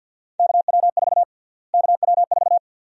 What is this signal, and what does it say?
Morse code, heard as a keyed tone, about 48 wpm, 700 Hz. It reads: KW4 KW4
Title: USA Call Sign Prefixes - Repeated 2x - 50wpm